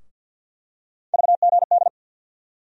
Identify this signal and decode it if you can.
Morse code, heard as a keyed tone, about 53 wpm, 700 Hz. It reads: VGD
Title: USA Call Sign Postfixes - 3 Letters 50wpm